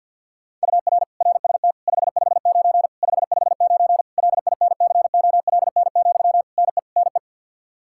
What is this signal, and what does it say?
Morse code, heard as a keyed tone, about 51 wpm, 700 Hz. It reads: UR RST 559 559 LINCOLN, NE NE
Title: Practice QSO - Exchange 1 - Beginning - 50wpm